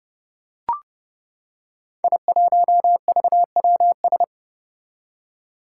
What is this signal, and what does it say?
Morse code, heard as a keyed tone, about 29 wpm, 700 Hz. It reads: I1VWS
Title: Five Challenging Letter-Number Combinations 30wpm